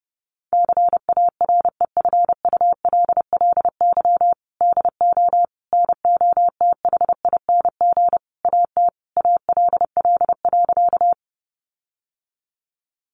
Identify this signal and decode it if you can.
Morse code, heard as a keyed tone, about 30 wpm, 700 Hz. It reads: CAREFULLY DO NOTHING AT ALL.
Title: Sentences from Top 1300 Words 30wpm